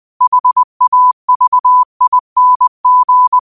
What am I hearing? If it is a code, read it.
Morse code, heard as a keyed tone, about 20 wpm, 1000 Hz. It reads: HAVING
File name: qotdfiles / 20wpm